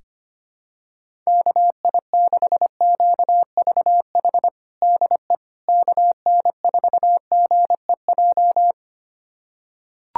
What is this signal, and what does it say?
Morse code, heard as a keyed tone, about 25 wpm, 700 Hz. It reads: KI6QVH DE KN4GEJ
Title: USA Callsign DE Callsign 25wpm